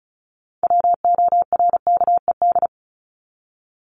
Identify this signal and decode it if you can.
Morse code, heard as a keyed tone, about 34 wpm, 700 Hz. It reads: WORKED
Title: ED Suffix - Top 100 Words with Base Words 35wpm